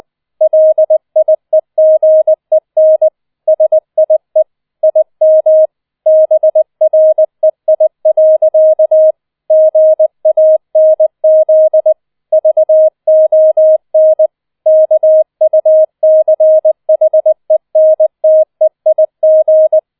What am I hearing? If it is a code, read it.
Morse code, heard as a keyed tone, about 20 wpm, 630 Hz. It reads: LIEGEN SIE IM BREI. GANZ VON KUCHENTEIG